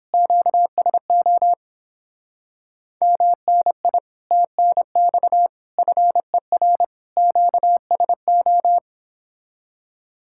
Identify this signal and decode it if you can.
Morse code, heard as a keyed tone, about 26 wpm, 700 Hz. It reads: QSO MNI TNX FER QSO
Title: Speed-Racing Common QSO Phrases - 22wpm